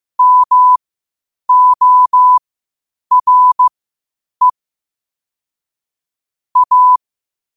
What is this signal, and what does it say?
Morse code, heard as a keyed tone, about 15 wpm, spaced as for 8 wpm, 1000 Hz. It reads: MORE A
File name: qotdfiles / 8wpm